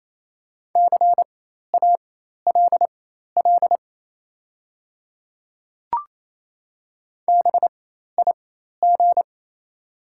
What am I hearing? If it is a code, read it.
Morse code, heard as a keyed tone, about 28 wpm, spaced as for 13 wpm, 700 Hz. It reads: CALL BIG
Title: 4x Character Spacing - Top 200 Words - 28wpm